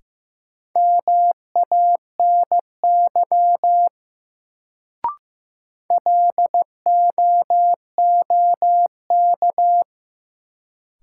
Morse code, heard as a keyed tone, about 15 wpm, 700 Hz. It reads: MANY LOOK
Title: Sentences from Top 100 Words - New Word Drill - 15wpm